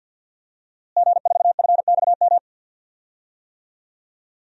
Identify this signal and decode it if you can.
Morse code, heard as a keyed tone, about 50 wpm, 700 Hz. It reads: G4FXM
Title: DXCC-100 Call Signs - 5 Characters - Repeated 2x - 50wpm